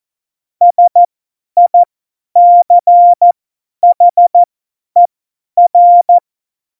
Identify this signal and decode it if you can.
Morse code, heard as a keyed tone, about 14 wpm, spaced as for 10 wpm, 700 Hz. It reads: SICHER
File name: Morsekurs / Witze / Witz 03. November 2020, 50 F70